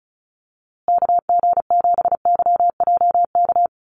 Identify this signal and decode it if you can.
Morse code, heard as a keyed tone, about 34 wpm, 700 Hz. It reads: KG7YJK
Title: USA Call Signs Repeated 3x 35wpm